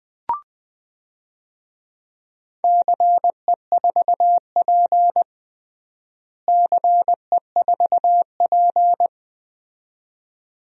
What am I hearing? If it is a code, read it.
Morse code, heard as a keyed tone, about 20 wpm, 700 Hz. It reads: CE4P CE4P
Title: DXCC-100 Call Signs - 4 Characters - Repeated 2x - 20wpm